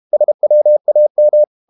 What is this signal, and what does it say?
Morse code, heard as a keyed tone, about 32 wpm, 600 Hz. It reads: SWAM
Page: Never 31wpm | CW med Gnister